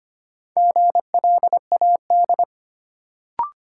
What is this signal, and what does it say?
Morse code, heard as a keyed tone, about 25 wpm, 700 Hz. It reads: GLAD
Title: Sentences from Top 800 Words - New Word Drill - 25wpm